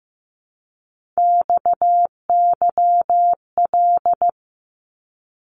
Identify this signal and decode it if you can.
Morse code, heard as a keyed tone, about 15 wpm, 700 Hz. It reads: XYL
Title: Abbreviations Repeated 3x - 15wpm